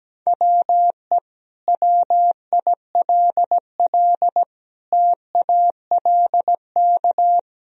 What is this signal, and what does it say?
Morse code, heard as a keyed tone, about 17 wpm, 700 Hz. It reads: WE WILL TALK
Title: Sentences from Top 400 Words and Abbreviations 17wpm